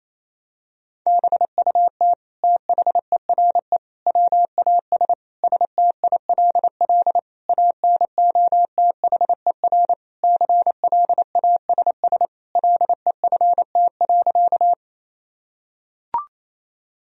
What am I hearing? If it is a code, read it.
Morse code, heard as a keyed tone, about 28 wpm, 700 Hz. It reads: BUT THERE WAS STILL ANOTHER CLASS LEFT.
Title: Sentences from Top 1000 Words - Review Part 2 - 28wpm